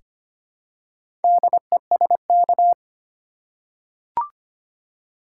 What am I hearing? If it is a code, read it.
Morse code, heard as a keyed tone, about 25 wpm, 700 Hz. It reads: DESK
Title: Four Letter Words - 25wpm